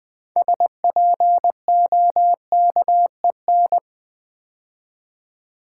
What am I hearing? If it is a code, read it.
Morse code, heard as a keyed tone, about 20 wpm, 700 Hz. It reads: SPOKEN
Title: Build A Word - 3 Letters At A Time - 6-Letter Words - 20wpm